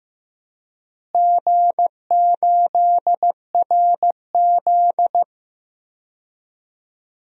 Morse code, heard as a keyed tone, about 15 wpm, 700 Hz. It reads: G8RZ
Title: DXCC-100 Call Signs - 4 Characters - Repeated 3x - 15wpm